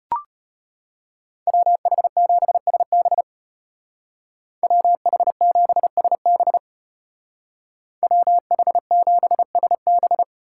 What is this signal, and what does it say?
Morse code, heard as a keyed tone, about 33 wpm, 700 Hz. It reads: WH7SB WH7SB WH7SB